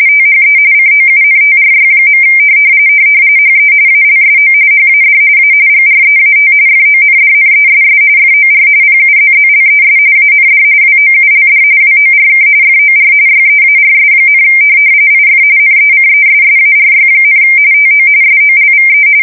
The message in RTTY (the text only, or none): THE COUNTRY. HE ALSO HAD A GREAT FONDNESS FOR PUZZLES AND WOULD WORK FOR YEARS, IF NECESSARY, TO SOLVE